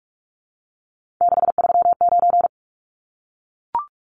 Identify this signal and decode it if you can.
Morse code, heard as a keyed tone, about 44 wpm, 700 Hz. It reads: 639
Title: Numbers - 3-Digit 45wpm